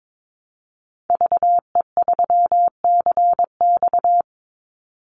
Morse code, heard as a keyed tone, about 22 wpm, 700 Hz. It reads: VE3CX
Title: Call Signs Repeated 2x - 22wpm